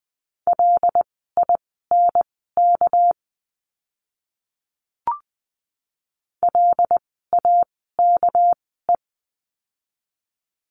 Morse code, heard as a keyed tone, about 20 wpm, spaced as for 14 wpm, 700 Hz. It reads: LINK LAKE
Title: 199 Lesson 32f - Letter k - Words 20wpm